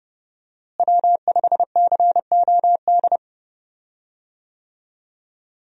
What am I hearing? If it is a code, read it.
Morse code, heard as a keyed tone, about 31 wpm, 700 Hz. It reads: W5COD
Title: USA Call Signs - 5 Characters - Repeated 3x - 30wpm